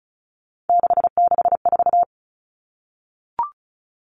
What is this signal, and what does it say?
Morse code, heard as a keyed tone, about 34 wpm, 700 Hz. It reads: 664
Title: Common Cut Numbers - 3-Digit 35wpm